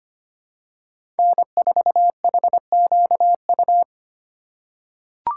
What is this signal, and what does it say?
Morse code, heard as a keyed tone, about 25 wpm, 700 Hz. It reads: N4HQU
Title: USA Call Signs - 6x Character Spacing - 25wpm